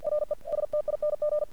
Morse code, heard as a keyed tone, about 47 wpm, 600 Hz. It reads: RESTING
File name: archived / CRSnet